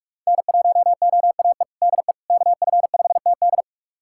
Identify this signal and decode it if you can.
Morse code, heard as a keyed tone, about 41 wpm, 700 Hz. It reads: N1OAE DE KF5TD
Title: USA Callsign DE Callsign - 5 Characters - 45wpm